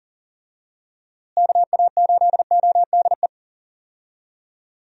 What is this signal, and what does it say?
Morse code, heard as a keyed tone, about 40 wpm, 700 Hz. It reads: KA8ODE
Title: USA Call Signs - 6 Characters - Repeated 2x - 40wpm